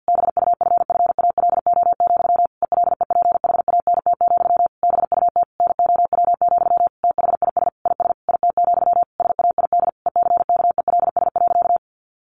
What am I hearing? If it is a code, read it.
Morse code, heard as a keyed tone, about 50 wpm, 700 Hz. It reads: BUFFALO, ELEPHANT, BUT NOW, THIS IS IT, SAID ECKELS.
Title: A Sound Of Thunder 50wpm